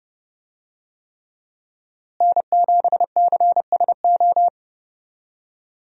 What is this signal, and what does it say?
Morse code, heard as a keyed tone, about 30 wpm, 700 Hz. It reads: N7CSO
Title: USA Call Signs - 5 Characters - Repeated 3x - 30wpm